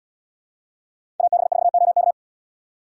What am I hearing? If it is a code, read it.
Morse code, heard as a keyed tone, about 78 wpm, 700 Hz. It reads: AB4OK